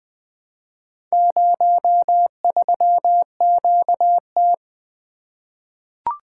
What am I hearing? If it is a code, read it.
Morse code, heard as a keyed tone, about 20 wpm, 700 Hz. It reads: 03QT
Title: Four Letter-Number Combinations - 20wpm